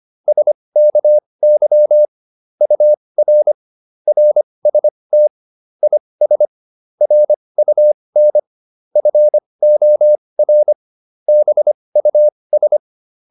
Morse code, heard as a keyed tone, about 25 wpm, spaced as for 20 wpm, 600 Hz. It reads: SKY UR RST IS RUN FOR BUS